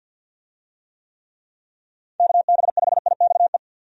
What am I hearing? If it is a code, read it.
Morse code, heard as a keyed tone, about 46 wpm, 700 Hz. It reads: KB5IXE
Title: USA Call Signs - 6 Characters - 50wpm